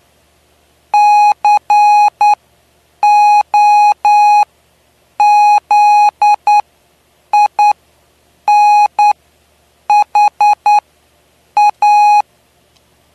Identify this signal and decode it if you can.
Morse code, heard as a keyed tone, about 9 wpm, spaced as for 7 wpm, 830 Hz. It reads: COZINHA
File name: cw / py2lcd / 09-PY2LCD-Li��o 9